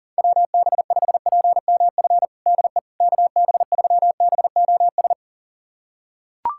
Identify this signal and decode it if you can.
Morse code, heard as a keyed tone, about 40 wpm, 700 Hz. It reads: WB5PMF DE KB3BOS
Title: USA Callsign DE Callsign 40wpm